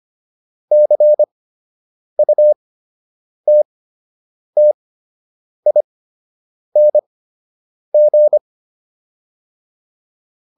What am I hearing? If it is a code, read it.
Morse code, heard as a keyed tone, about 25 wpm, spaced as for 8 wpm, 600 Hz. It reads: CUTTING